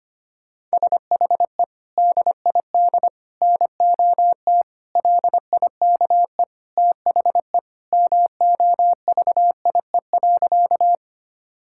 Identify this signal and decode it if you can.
Morse code, heard as a keyed tone, about 25 wpm, 700 Hz. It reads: SHE DID NOT LIKE THE MOVIE.
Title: Sentences from Top 1000 Words - Review Part 5 - 25wpm